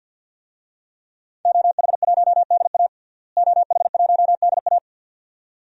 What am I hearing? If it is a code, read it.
Morse code, heard as a keyed tone, about 52 wpm, 700 Hz. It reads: OH1DA OH1DA
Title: DXCC-100 Call Signs - 5 Characters - Repeated 2x - 50wpm